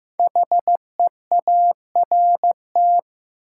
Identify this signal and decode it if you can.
Morse code, heard as a keyed tone, about 15 wpm, 700 Hz. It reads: HEART
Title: Sets of 3 Words - New Word Drill - 15wpm